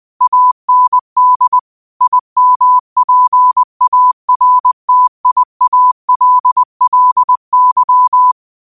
Morse code, heard as a keyed tone, about 20 wpm, 1000 Hz. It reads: AND IMPARTIALLY
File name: qotdfiles / 20wpm